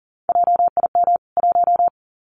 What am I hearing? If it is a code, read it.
Morse code, heard as a keyed tone, about 40 wpm, 700 Hz. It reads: JIM 1
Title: CWT Exchange 40wpm